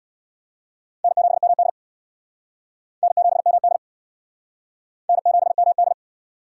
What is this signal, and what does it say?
Morse code, heard as a keyed tone, about 63 wpm, 700 Hz. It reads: N7MD N7MD N7MD